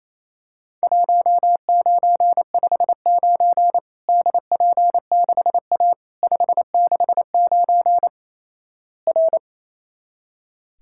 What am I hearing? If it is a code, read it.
Morse code, heard as a keyed tone, about 28 wpm, 700 Hz. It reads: 1959 DP6A 569 R
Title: Worked All Europe DX Contest - QTC Exchange - 28wpm